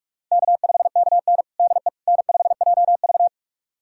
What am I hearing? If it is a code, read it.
Morse code, heard as a keyed tone, about 44 wpm, 700 Hz. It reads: K5KN DE N5JV
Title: USA Callsign DE Callsign - 4 Characters - 45wpm